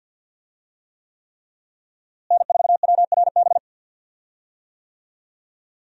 Morse code, heard as a keyed tone, about 52 wpm, 700 Hz. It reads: N4WRB
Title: USA Call Signs - 5 Characters - Repeated 3x - 50wpm